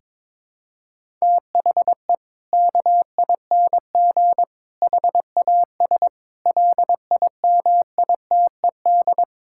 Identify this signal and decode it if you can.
Morse code, heard as a keyed tone, about 22 wpm, 700 Hz. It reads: THE KING HAS LIMITED
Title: Sentences from Top 1000 Words - Review Part 6 - 22wpm